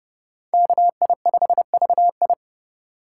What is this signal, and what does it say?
Morse code, heard as a keyed tone, about 30 wpm, 700 Hz. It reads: KI5VI